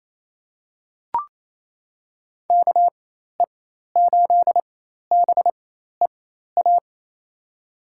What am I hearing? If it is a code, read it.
Morse code, heard as a keyed tone, about 28 wpm, spaced as for 13 wpm, 700 Hz. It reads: KE8BEA
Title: USA Call Signs - 6 Characters - 4x Character Spacing - 28wpm